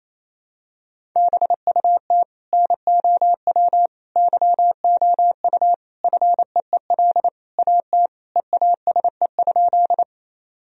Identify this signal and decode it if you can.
Morse code, heard as a keyed tone, about 28 wpm, 700 Hz. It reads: BUT NOW YOU FEEL AT EASE?
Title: Sentences from Top 1000 Words - Review Part 1 - 28wpm